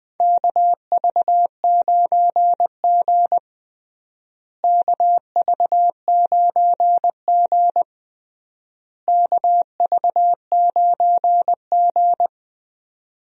Morse code, heard as a keyed tone, about 20 wpm, 700 Hz. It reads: KV9G KV9G KV9G
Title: USA Call Signs - 4 Characters - Repeated 3x - 20wpm